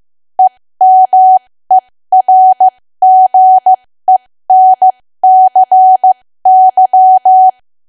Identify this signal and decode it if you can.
Morse code, heard as a keyed tone, about 15 wpm, spaced as for 13 wpm, 750 Hz. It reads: EMERGENCY
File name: cwa / W1AW-Level2 / 13WPM